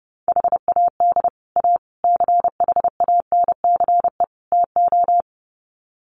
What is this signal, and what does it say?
Morse code, heard as a keyed tone, about 28 wpm, 700 Hz. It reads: HAD A CHANCE TO
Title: Sets of 4 Words 30wpm